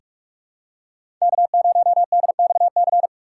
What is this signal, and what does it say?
Morse code, heard as a keyed tone, about 44 wpm, 700 Hz. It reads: K0DXC
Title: K1USN SST QSO Practice - 45wpm